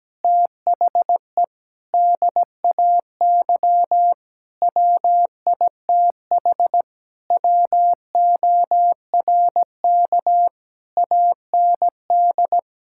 Morse code, heard as a keyed tone, about 17 wpm, 700 Hz. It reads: THE DAY WITH WORK AND
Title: Sentences from Top 1000 Words - Review Part 6 - 17wpm